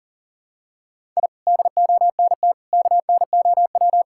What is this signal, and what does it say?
Morse code, heard as a keyed tone, about 37 wpm, 700 Hz. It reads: I DONT KNOW
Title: Sentences from Top 1000 Words - Review Part 5 - 40wpm